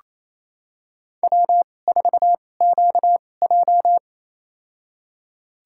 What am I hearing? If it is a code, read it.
Morse code, heard as a keyed tone, about 28 wpm, spaced as for 20 wpm, 700 Hz. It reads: W4QJ
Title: USA Call Signs - 4 Characters - 2x Character Spacing - 28wpm